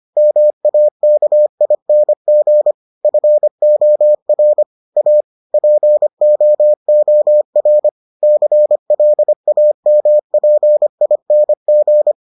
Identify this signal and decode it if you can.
Morse code, heard as a keyed tone, about 25 wpm, 600 Hz. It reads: MAKING FOR A POOR CLAMPING